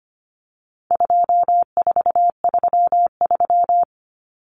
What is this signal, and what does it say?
Morse code, heard as a keyed tone, about 25 wpm, 700 Hz. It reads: 2433
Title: Common Cut Numbers - 4-Digit 25wpm